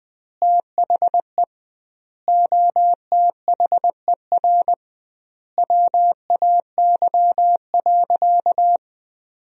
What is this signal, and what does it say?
Morse code, heard as a keyed tone, about 21 wpm, 700 Hz. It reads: THE OTHER WAY.
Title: 2x Word-Spacing - Sentences from Top 100 Words - 20wpm